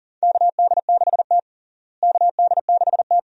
Morse code, heard as a keyed tone, about 40 wpm, 700 Hz. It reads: KD6T KD6T
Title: USA Call Signs - 4 Characters - Repeated 2x - 40wpm